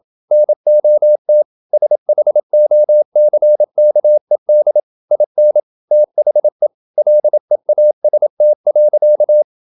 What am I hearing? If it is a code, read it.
Morse code, heard as a keyed tone, about 27 wpm, 600 Hz. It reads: NOT SHOCKED IN THE LEAST.
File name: Never 0027wpm